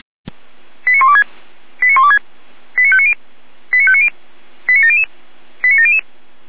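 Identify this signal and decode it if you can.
Five-tone selective call (ZVEI), heard as quick runs of five tones, tones of about 70 ms. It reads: 89126 89126 89590 89590 89700 89700
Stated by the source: ZVEI five-tone call 89120 received by SDR.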